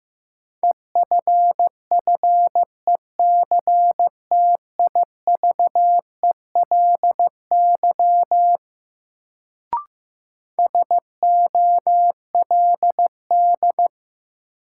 Morse code, heard as a keyed tone, about 15 wpm, 700 Hz. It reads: EFFECTIVELY SOLD
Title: Sentences from Top 1400 Words - New Word Drill - 15wpm